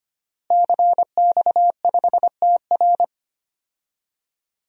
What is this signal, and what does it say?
Morse code, heard as a keyed tone, about 25 wpm, 700 Hz. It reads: CX5TR
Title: DXCC-100 Call Signs - 5 Characters - Repeated 3x - 25wpm